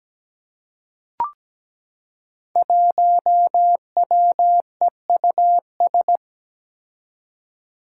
Morse code, heard as a keyed tone, about 17 wpm, 700 Hz. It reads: 1WEUS
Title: Five Challenging Letter-Number Combinations 17wpm